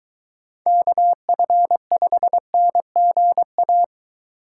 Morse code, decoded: KF5NGA

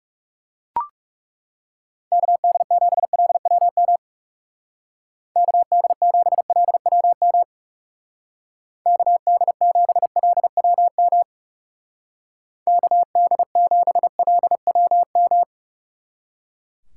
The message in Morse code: KD7LWM KD7LWM KD7LWM KD7LWM

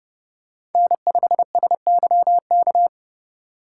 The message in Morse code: N5SYK

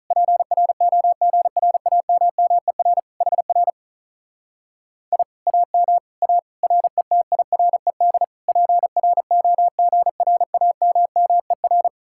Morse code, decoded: PROGRAMMER HR I AM A RETIRED PROGRAMMER

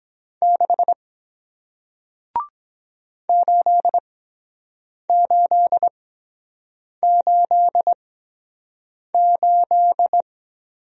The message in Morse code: 6 8888

